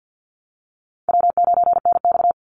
Morse code, heard as a keyed tone, about 50 wpm, 700 Hz. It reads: W9NX
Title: Easy Call Signs Repeated 3x - 50wpm